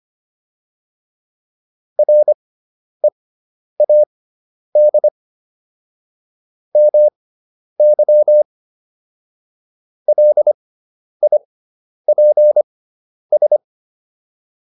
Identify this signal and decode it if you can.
Morse code, heard as a keyed tone, about 25 wpm, spaced as for 10 wpm, 600 Hz. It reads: READ MY LIPS